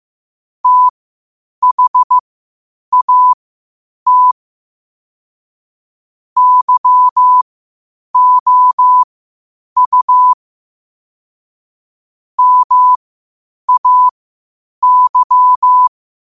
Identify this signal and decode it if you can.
Morse code, heard as a keyed tone, about 15 wpm, spaced as for 8 wpm, 1000 Hz. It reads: THAT YOU MAY
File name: qotdfiles / 8wpm